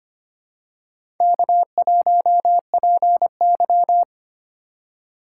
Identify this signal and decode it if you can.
Morse code, heard as a keyed tone, about 25 wpm, 700 Hz. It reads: K1PY
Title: USA Call Signs - 4 Characters - 8x Character Spacing - 25wpm